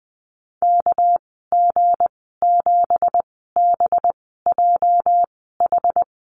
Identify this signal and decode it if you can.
Morse code, heard as a keyed tone, about 20 wpm, spaced as for 15 wpm, 700 Hz. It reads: KG7BJH
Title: 231 Lesson 37g - Letter x - Call Signs 20wpm